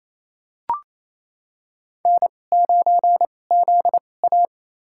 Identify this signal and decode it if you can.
Morse code, heard as a keyed tone, about 27 wpm, spaced as for 20 wpm, 700 Hz. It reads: N9ZA
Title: USA Call Signs - 4 Characters - 2x Character Spacing - 28wpm